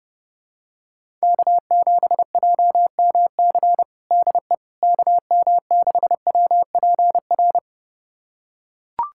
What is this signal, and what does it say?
Morse code, heard as a keyed tone, about 32 wpm, 700 Hz. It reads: K7JMC DE KM6WPR
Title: USA Callsign DE Callsign 30wpm